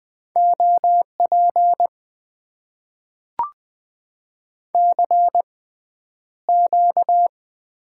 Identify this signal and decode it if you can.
Morse code, heard as a keyed tone, about 20 wpm, spaced as for 10 wpm, 700 Hz. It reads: OP CQ
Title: Top 90 QSO Elements - 6x Character Spacing - 20wpm